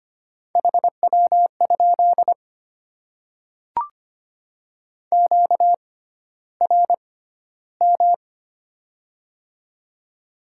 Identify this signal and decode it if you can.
Morse code, heard as a keyed tone, about 25 wpm, spaced as for 13 wpm, 700 Hz. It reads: HW? QRM
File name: Top 90 QSO Elements - 6x Character Spacing - 25wpm